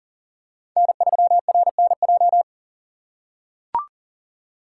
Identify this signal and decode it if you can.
Morse code, heard as a keyed tone, about 40 wpm, 700 Hz. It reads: N3RNJ
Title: USA Call Signs - 5 Characters - 2x Character Spacing - 40wpm